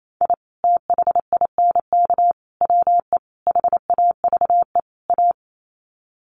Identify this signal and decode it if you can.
Morse code, heard as a keyed tone, about 28 wpm, 700 Hz. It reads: I THINK WE HAVE A
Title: Sets of 5 Words Encore 28wpm